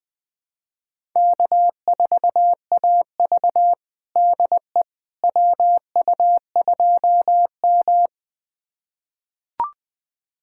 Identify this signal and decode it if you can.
Morse code, heard as a keyed tone, about 20 wpm, 700 Hz. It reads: K4AV DE WU2M
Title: USA Callsign DE Callsign - 4 Characters - 20wpm